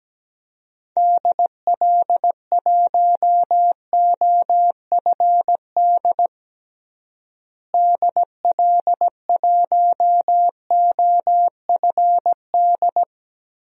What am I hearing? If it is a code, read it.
Morse code, heard as a keyed tone, about 17 wpm, 700 Hz. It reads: DL1OFD DL1OFD
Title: DXCC-100 Call Signs - 6 Characters - Repeated 3x - 17wpm